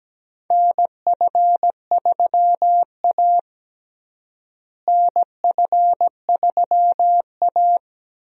Morse code, heard as a keyed tone, about 17 wpm, 700 Hz. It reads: NF3A NF3A
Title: USA Call Signs - 4 Characters - Repeated 2x - 17wpm